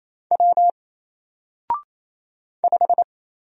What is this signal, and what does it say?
Morse code, heard as a keyed tone, about 27 wpm, 700 Hz. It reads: W 5
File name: Single-Letter-Number-28wpm